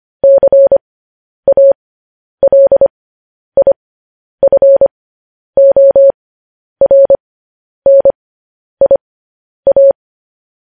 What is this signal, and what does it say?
Morse code, heard as a keyed tone, about 25 wpm, spaced as for 10 wpm, 550 Hz. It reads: CALIFORNIA